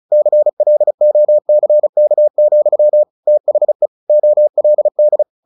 Morse code, heard as a keyed tone, about 34 wpm, 600 Hz. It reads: CLOCK, THE OLD
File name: the old man and the sea / the old man and the sea 1 35wpm